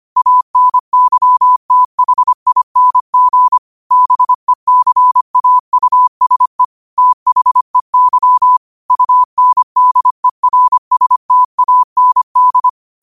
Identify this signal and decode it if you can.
Morse code, heard as a keyed tone, about 25 wpm, 1000 Hz. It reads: ANYTHING BECAUSE THEY UNDERSTAND
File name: qotdfiles / 25wpm